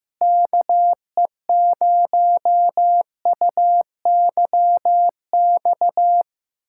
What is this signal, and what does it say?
Morse code, heard as a keyed tone, about 15 wpm, 700 Hz. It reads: KE0UYX
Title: USA Callsign DE Callsign - 6 Characters - 15wpm